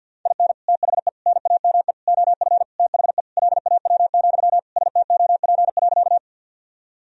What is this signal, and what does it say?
Morse code, heard as a keyed tone, about 52 wpm, 700 Hz. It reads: IN THE NAME OF THE LAW, STOP.
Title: Sentences from Top 1000 Words - Review Part 3 - 50wpm